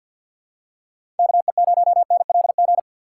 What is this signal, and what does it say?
Morse code, heard as a keyed tone, about 52 wpm, 700 Hz. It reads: KE0NLG
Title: USA Call Signs - 6 Characters - 6x Character Spacing - 50wpm